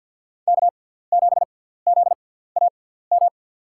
Morse code, heard as a keyed tone, about 51 wpm, spaced as for 17 wpm, 700 Hz. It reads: K7ZAM
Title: USA Call Signs - 5 Characters - 6x Character Spacing - 50wpm